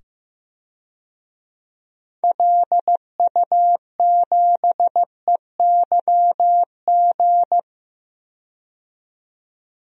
Morse code, heard as a keyed tone, about 15 wpm, 700 Hz. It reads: LU7EYG
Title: DXCC-100 Call Signs - 6 Characters - 15wpm